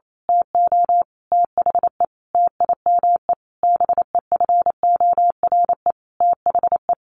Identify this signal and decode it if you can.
Morse code, heard as a keyed tone, about 28 wpm, 700 Hz. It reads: TO THE TIME BEFORE THE
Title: A Sound Of Thunder 28wpm